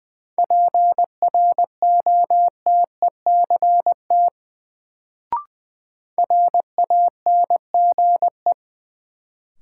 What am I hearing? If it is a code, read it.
Morse code, heard as a keyed tone, about 20 wpm, 700 Hz. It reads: PROTECT RANGE
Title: Sentences from Top 900 Words - New Word Drill - 20wpm